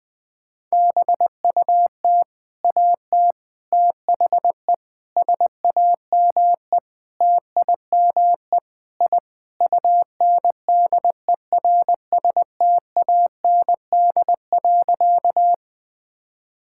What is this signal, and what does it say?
Morse code, heard as a keyed tone, about 20 wpm, 700 Hz. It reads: BUT AT THE SAME TIME I UNDERSTAND.
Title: Sentences with 4-Word Phrases - 20wpm